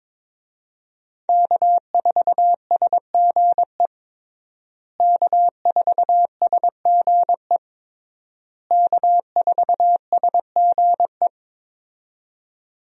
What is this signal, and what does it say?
Morse code, heard as a keyed tone, about 22 wpm, 700 Hz. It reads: K4SGE K4SGE K4SGE
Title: USA Call Signs - 5 Characters - Repeated 3x - 22wpm